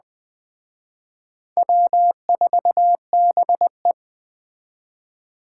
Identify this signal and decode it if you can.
Morse code, heard as a keyed tone, about 21 wpm, 700 Hz. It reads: W4BE